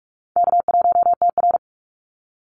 Morse code, heard as a keyed tone, about 44 wpm, 700 Hz. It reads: K1TR
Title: Easy Call Signs - 45wpm